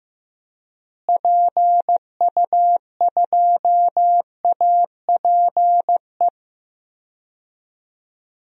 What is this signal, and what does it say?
Morse code, heard as a keyed tone, about 15 wpm, 700 Hz. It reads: PU2APE
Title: DXCC-100 Call Signs - 6 Characters - Repeated 3x - 15wpm